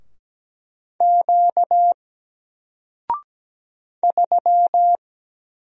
Morse code, heard as a keyed tone, about 17 wpm, 700 Hz. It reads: Q 3
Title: Single Letter-Number 17wpm